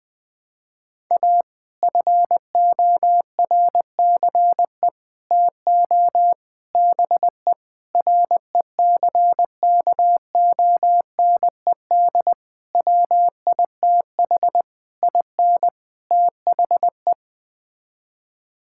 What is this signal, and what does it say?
Morse code, heard as a keyed tone, about 20 wpm, 700 Hz. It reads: A FORCE TO BE RECKONED WITH IN THE
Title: Sets of 8 Words - 20wpm